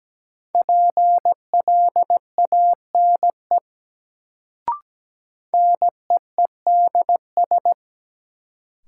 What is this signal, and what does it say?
Morse code, heard as a keyed tone, about 17 wpm, 700 Hz. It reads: PLANE NEEDS
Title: Sentences from Top 1000 Words - Review Part 3 - New Word Drill - 17wpm